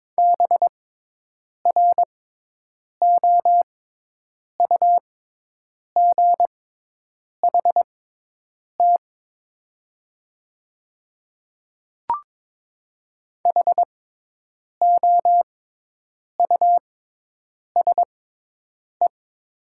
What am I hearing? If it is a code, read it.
Morse code, heard as a keyed tone, about 22 wpm, spaced as for 8 wpm, 700 Hz. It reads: BROUGHT HOUSE
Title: 6x Character Spacing - Top 500 Words - 22wpm